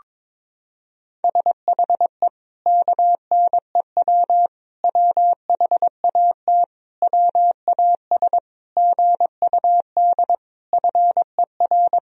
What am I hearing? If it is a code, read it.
Morse code, heard as a keyed tone, about 22 wpm, 700 Hz. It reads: SHE KNEW WHAT WAS GUD FER
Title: Sentences from Top 400 Words and Abbreviations 22wpm